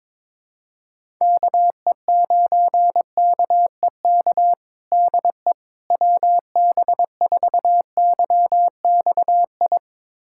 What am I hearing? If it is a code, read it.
Morse code, heard as a keyed tone, about 22 wpm, 700 Hz. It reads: KE9KEK DE WB4YXI
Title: USA Callsign DE Callsign - 6 Characters - 22wpm